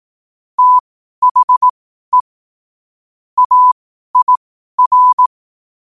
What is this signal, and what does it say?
Morse code, heard as a keyed tone, about 18 wpm, spaced as for 12 wpm, 1000 Hz. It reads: THE AIR
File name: qotdfiles / 12wpm